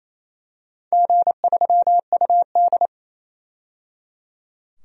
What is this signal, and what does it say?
Morse code, heard as a keyed tone, about 28 wpm, 700 Hz. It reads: G3UD
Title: DXCC-100 Call Signs - 4 Characters - 28wpm